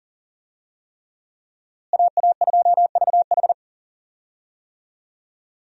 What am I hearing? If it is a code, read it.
Morse code, heard as a keyed tone, about 39 wpm, 700 Hz. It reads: AA2VH
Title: USA Call Signs - 5 Characters - Repeated 3x - 40wpm